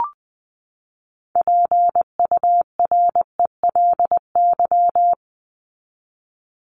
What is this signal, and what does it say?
Morse code, heard as a keyed tone, about 20 wpm, 700 Hz. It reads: PURELY
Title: LY Suffix - Top 100 Words with Base Words 20wpm